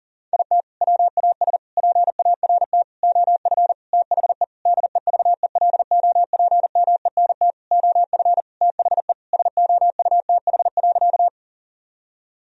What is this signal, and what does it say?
Morse code, heard as a keyed tone, about 40 wpm, 700 Hz. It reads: IT WAS PART OF THE DEVELOPMENT OF THE SOUTH.
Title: Sentences from Top 1000 Words - Review Part 4 - 40wpm